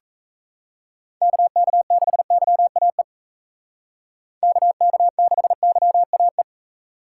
Morse code, heard as a keyed tone, about 36 wpm, 700 Hz. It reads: KK6YAE KK6YAE